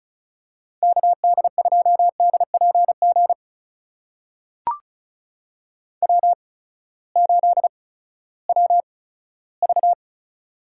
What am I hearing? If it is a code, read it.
Morse code, heard as a keyed tone, about 36 wpm, spaced as for 18 wpm, 700 Hz. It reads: KD2DPG W8WV
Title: USA Call Signs - 8x Character Spacing - 35wpm